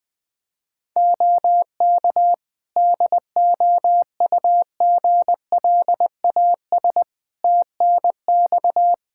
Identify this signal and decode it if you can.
Morse code, heard as a keyed tone, about 20 wpm, 700 Hz. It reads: OK DOUGLAS TNX
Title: Practice QSO - Exchange 4 - Ending - 20wpm